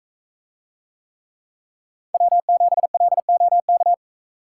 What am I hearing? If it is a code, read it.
Morse code, heard as a keyed tone, about 43 wpm, 700 Hz. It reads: W7LOK